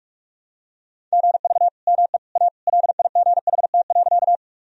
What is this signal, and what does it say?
Morse code, heard as a keyed tone, about 46 wpm, 700 Hz. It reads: GV ME A LIGHT.
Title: Sentences from Top 200 Words and Abbreviations 45wpm